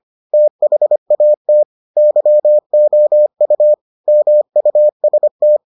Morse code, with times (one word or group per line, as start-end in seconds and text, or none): 0.3-1.6 THAT
2.0-3.7 YOU
4.1-5.6 MUST